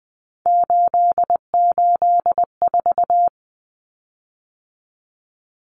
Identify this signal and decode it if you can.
Morse code, heard as a keyed tone, about 20 wpm, 700 Hz. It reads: 884
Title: Common Cut Numbers - 3-Digit 20wpm